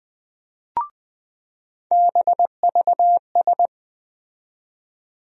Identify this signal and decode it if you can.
Morse code, heard as a keyed tone, about 20 wpm, 700 Hz. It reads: BVS
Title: USA Call Sign Postfixes - 3 Letters 20wpm